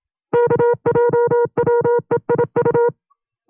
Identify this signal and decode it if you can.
Morse code, heard as a keyed tone, about 27 wpm, 480 Hz. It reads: KJWEIU
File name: FreeFiles / CW-practice / 25WPM-side1